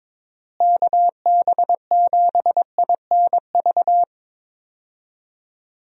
Morse code, heard as a keyed tone, about 23 wpm, 700 Hz. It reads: KB7INV
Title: USA Call Signs - 6 Characters - Repeated 2x - 22wpm